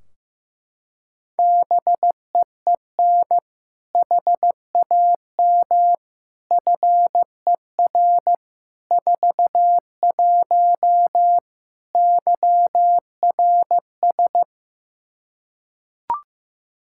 Morse code, heard as a keyed tone, about 15 wpm, 700 Hz. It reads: BEEN HAM FER 41 YRS
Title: Practice QSO - Exchange 3 - More Info - 15wpm